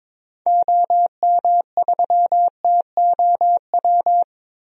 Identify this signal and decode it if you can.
Morse code, heard as a keyed tone, about 22 wpm, 700 Hz. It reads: OM3TOW
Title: DXCC-100 Call Signs - 6 Characters - Repeated 2x - 22wpm